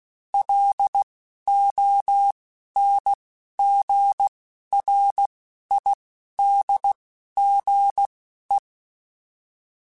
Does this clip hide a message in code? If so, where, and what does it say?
Yes: Morse code from 0.3 to 8.6 s, heard as a keyed tone, about 16 wpm, spaced as for 11 wpm, 800 Hz. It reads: LONGRIDGE